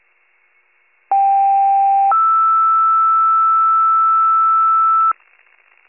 Two-tone page, heard as a one-second tone, then a three-second tone, about 790 then 1320 Hz. It is heard from 1.1 to 5.1 s.